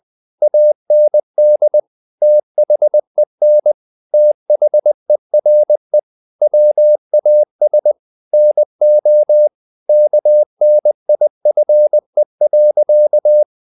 Morse code, heard as a keyed tone, about 20 wpm, 600 Hz. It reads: AND THEN THERE WAS NO KNIFE.
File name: the old man and the sea part 2 20wpm